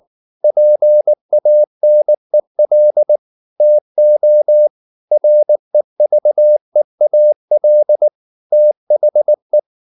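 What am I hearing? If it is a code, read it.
Morse code, heard as a keyed tone, about 19 wpm, 600 Hz. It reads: PANEL TO REVEAL THE
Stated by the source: Morse code at 19 wpm from CW med Gnister, reading Ken Follett's novel Never in English.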